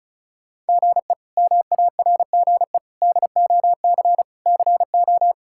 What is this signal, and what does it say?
Morse code, heard as a keyed tone, about 36 wpm, 700 Hz. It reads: GE MARGE DOC CO
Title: K1USN SST QSO Practice - 35wpm